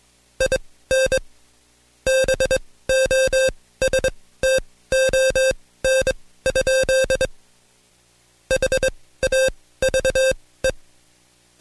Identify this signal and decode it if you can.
Morse code, heard as a keyed tone, about 22 wpm, spaced as for 16 wpm, 530 Hz. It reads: IN BOSTON? HAVE
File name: Code Quick / AudioDownld2 / Seg134-Stepper7 15WPM